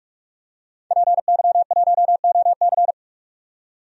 Morse code, heard as a keyed tone, about 47 wpm, 700 Hz. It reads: PY1OC
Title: DXCC-100 Call Signs - 5 Characters - 45wpm